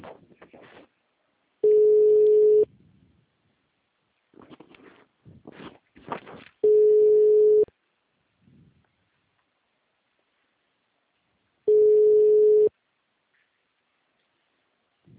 Ringback tone (Europe, 425 Hz).
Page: Piracicaba - SP